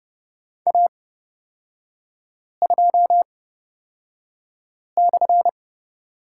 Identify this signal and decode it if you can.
Morse code, heard as a keyed tone, about 30 wpm, spaced as for 5 wpm, 700 Hz. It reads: A2/
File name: Lesson 36e - BT - Three Characters 30wpm